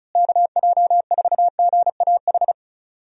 Morse code, heard as a keyed tone, about 34 wpm, 700 Hz. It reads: KJ4GAH
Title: USA Call Signs - 6 Characters - Repeated 3x - 35wpm